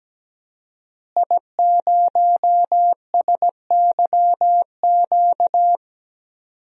Morse code, heard as a keyed tone, about 17 wpm, 700 Hz. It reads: I0SYQ
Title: DXCC-100 Call Signs - 5 Characters - Repeated 3x - 17wpm